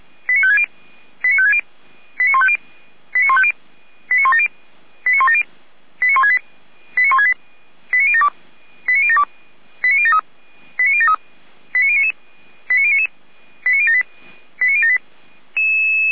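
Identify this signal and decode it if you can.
Five-tone selective call (ZVEI), heard as quick runs of five tones, tones of about 70 ms. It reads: 89560 89560 89150 89150 89170 89170 89168 89168 89072 89072 89073 89073 89099 89099 89078 89078
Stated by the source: ZVEI five-tone selective call 89167 received with an SDR.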